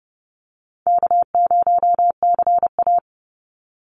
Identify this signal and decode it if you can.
Morse code, heard as a keyed tone, about 30 wpm, 700 Hz. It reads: K0CA
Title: Easy Call Signs - 30wpm